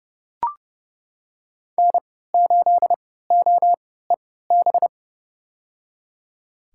Morse code, heard as a keyed tone, about 31 wpm, spaced as for 17 wpm, 700 Hz. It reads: N8OEB
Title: USA Call Signs - 5 Characters - 3x Character Spacing - 30wpm